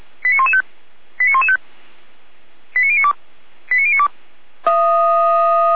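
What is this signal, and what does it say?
Five-tone selective call (ZVEI), heard as quick runs of five tones, tones of about 80 ms. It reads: 89115 89115 89092 89092
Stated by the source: Received on SDR, ZVEI five-tone call 89080.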